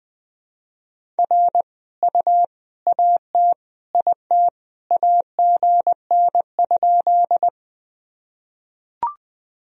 Morse code, heard as a keyed tone, about 20 wpm, 700 Hz. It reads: R U AT IT AGN?
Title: Sentences from Top 200 Words and Abbreviations 20wpm